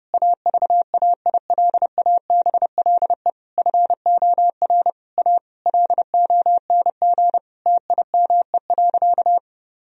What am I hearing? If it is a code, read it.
Morse code, heard as a keyed tone, about 30 wpm, 700 Hz. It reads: AVAILABLE FOR A LONG TIME.
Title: Sentences from Top 1000 Words - Review Part 4 - 30wpm